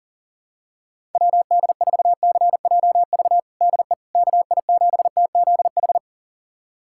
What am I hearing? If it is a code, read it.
Morse code, heard as a keyed tone, about 40 wpm, 690 Hz. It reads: WD4CJV DE KI7TZH
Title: USA Callsign DE Callsign - 6 Characters - 40wpm